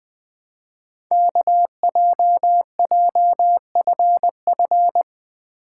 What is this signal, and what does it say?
Morse code, heard as a keyed tone, about 20 wpm, 700 Hz. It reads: KJJFF
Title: Code Groups - 5 Letters - 20wpm